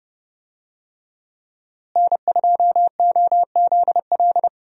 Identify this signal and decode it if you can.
Morse code, heard as a keyed tone, about 30 wpm, 700 Hz. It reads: N2OZL
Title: USA Call Signs - 5 Characters - Repeated 3x - 30wpm